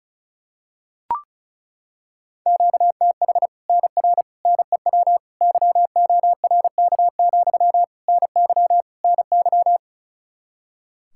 Morse code, heard as a keyed tone, about 34 wpm, 700 Hz. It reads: QTH NR NEW YORK, NY NY
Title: Practice QSO - Exchange 1 - Beginning - 35wpm